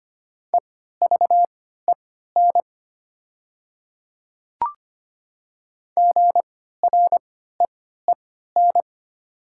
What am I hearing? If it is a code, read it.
Morse code, heard as a keyed tone, about 25 wpm, spaced as for 14 wpm, 700 Hz. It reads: EVEN GREEN
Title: 3x Character Spacing - Top 500 Words - 25wpm